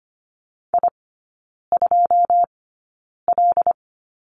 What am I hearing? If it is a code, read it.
Morse code, heard as a keyed tone, about 25 wpm, spaced as for 9 wpm, 700 Hz. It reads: I2L
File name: Lesson 14e - Number 2 - Three Characters 25wpm